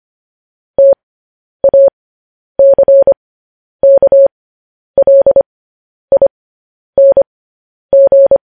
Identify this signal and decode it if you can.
Morse code, heard as a keyed tone, about 25 wpm, spaced as for 10 wpm, 550 Hz. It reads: TACKLING
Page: news-headlines-in-morse-code-at-10-wpm